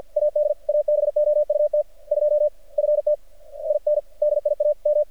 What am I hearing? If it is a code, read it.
Morse code, heard as a keyed tone, about 48 wpm, 600 Hz. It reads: AD ABOUT 2 FT IN DIAM
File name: archived / CRSnet